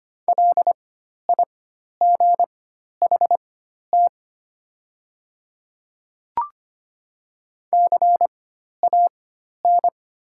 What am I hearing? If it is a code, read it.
Morse code, heard as a keyed tone, about 25 wpm, spaced as for 12 wpm, 700 Hz. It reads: LIGHT CAN
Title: 4x Character Spacing - Top 300 Words - 25wpm